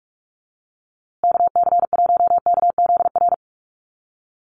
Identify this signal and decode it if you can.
Morse code, heard as a keyed tone, about 44 wpm, 700 Hz. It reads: KC1KZR
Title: USA Call Signs Repeated 2x 45wpm